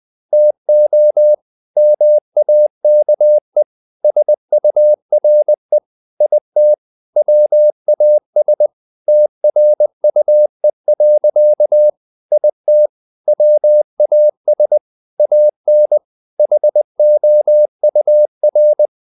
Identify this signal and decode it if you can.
Morse code, heard as a keyed tone, about 20 wpm, 600 Hz. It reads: TO MAKE SURE IT WAS TRUE. IT WAS AN HOUR
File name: the old man and the sea part 2 20wpm